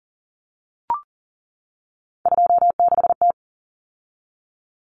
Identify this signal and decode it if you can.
Morse code, heard as a keyed tone, about 40 wpm, 700 Hz. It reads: 26T
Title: Common Cut Numbers - 3-Digit 40wpm